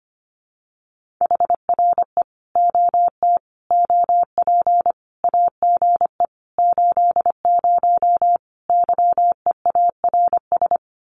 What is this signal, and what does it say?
Morse code, heard as a keyed tone, about 25 wpm, 700 Hz. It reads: HRE OT OP AGE 80 YEARS
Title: Common QSO Phrases 25wpm